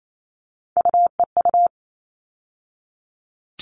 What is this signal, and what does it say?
Morse code, heard as a keyed tone, about 28 wpm, 700 Hz. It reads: UEU
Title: All Cut Numbers - 3-Digit 28wpm